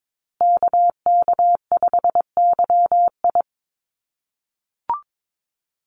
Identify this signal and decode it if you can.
Morse code, heard as a keyed tone, about 22 wpm, 700 Hz. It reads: KK5YI
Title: USA Call Signs Repeated 2x 22wpm